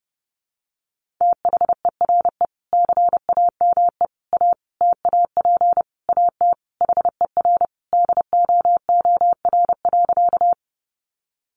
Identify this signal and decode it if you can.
Morse code, heard as a keyed tone, about 30 wpm, 700 Hz. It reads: THERE CAME A TAP AT HER DOOR.
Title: Sentences from Top 1400 Words 30wpm